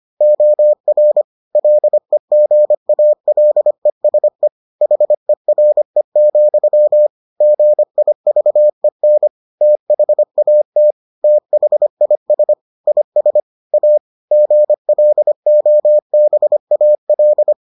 Morse code, heard as a keyed tone, about 25 wpm, 600 Hz. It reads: OR LEGALESE HERE, GIVEN THAT THIS IS A GLOBAL